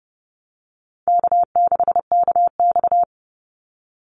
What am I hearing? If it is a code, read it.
Morse code, heard as a keyed tone, about 29 wpm, 700 Hz. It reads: K6KX
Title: Easy Call Signs Repeated 3x - 30wpm